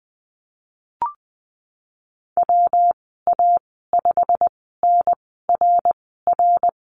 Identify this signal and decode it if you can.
Morse code, heard as a keyed tone, about 20 wpm, spaced as for 14 wpm, 700 Hz. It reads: WA5NRR
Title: 161 Lesson 26g - Letter g - Call Signs 20wpm